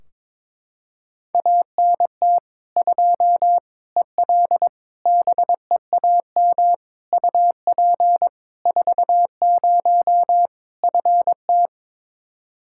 Morse code, heard as a keyed tone, about 22 wpm, 700 Hz. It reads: ANT 2 EL BEAM UP 40 FT
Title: Practice QSO - Exchange 2 - Info - 22wpm